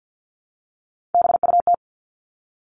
Morse code, heard as a keyed tone, about 50 wpm, 700 Hz. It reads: BUT
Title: Top 500 Words 50wpm